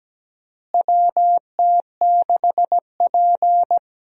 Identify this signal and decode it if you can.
Morse code, heard as a keyed tone, about 17 wpm, 700 Hz. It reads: WT6P
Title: USA Call Signs - 4 Characters - Repeated 2x - 17wpm